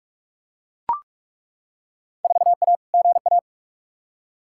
Morse code, heard as a keyed tone, about 41 wpm, 700 Hz. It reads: 4A GA
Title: Field Day Exchanges - 45wpm